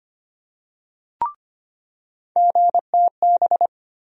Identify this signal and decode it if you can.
Morse code, heard as a keyed tone, about 25 wpm, 700 Hz. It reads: GTB
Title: USA Call Sign Postfixes - 3 Letters 25wpm